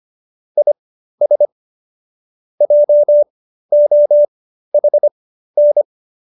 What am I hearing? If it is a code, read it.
Morse code, heard as a keyed tone, about 25 wpm, spaced as for 13 wpm, 590 Hz. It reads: IS JOHN